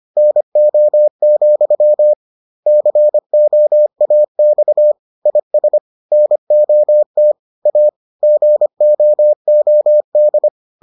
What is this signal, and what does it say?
Morse code, heard as a keyed tone, about 25 wpm, 600 Hz. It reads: NO, COAX IS NOT A GOOD